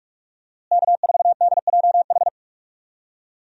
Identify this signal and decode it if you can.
Morse code, heard as a keyed tone, about 41 wpm, 700 Hz. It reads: K4DJH
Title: USA Call Signs - 5 Characters - Repeated 2x - 45wpm